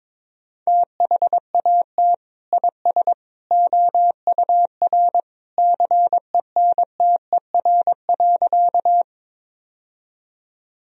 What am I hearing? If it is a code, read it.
Morse code, heard as a keyed tone, about 22 wpm, 700 Hz. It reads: THAT IS OUR CENTER.
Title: Sentences from Top 1000 Words - Review Part 2 - 22wpm